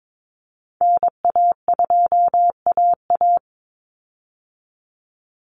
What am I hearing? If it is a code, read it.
Morse code, heard as a keyed tone, about 22 wpm, 700 Hz. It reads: NA2AA
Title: Call Signs Repeated 2x - 22wpm